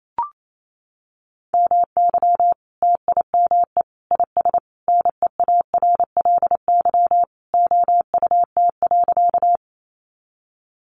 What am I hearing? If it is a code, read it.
Morse code, heard as a keyed tone, about 28 wpm, 700 Hz. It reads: MY TIME IS NEARLY OUT.
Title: Sentences from Top 1300 Words 28wpm